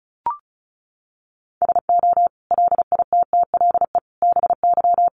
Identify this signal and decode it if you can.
Morse code, heard as a keyed tone, about 34 wpm, 700 Hz. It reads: SO LITTLE BY
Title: Sentences from Top 1500 Words 35wpm